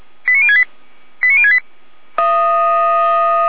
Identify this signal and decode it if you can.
Five-tone selective call (ZVEI), heard as quick runs of five tones, tones of about 70 ms. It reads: 89068 89068